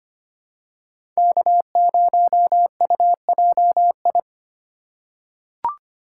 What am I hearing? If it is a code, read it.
Morse code, heard as a keyed tone, about 25 wpm, 700 Hz. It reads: K0UJI
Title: USA Call Signs - 5 Characters - 4x Character Spacing - 25wpm